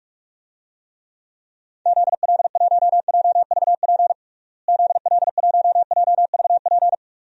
Morse code, heard as a keyed tone, about 41 wpm, 700 Hz. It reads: ZL1JVP ZL1JVP
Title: DXCC-100 Call Signs - 6 Characters - Repeated 3x - 45wpm